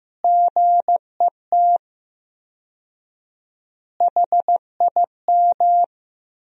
Morse code, heard as a keyed tone, about 15 wpm, 700 Hz. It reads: GET HIM
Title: 4x Word-Spacing - Sentences from Top 400 Words - 15wpm